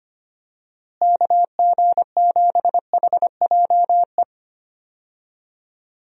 Morse code, heard as a keyed tone, about 25 wpm, 700 Hz. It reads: KG7HJE
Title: USA Call Signs - 6 Characters - Repeated 3x - 25wpm